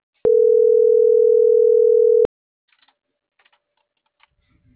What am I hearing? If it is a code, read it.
Ringback tone (North America), 440 + 480 Hz.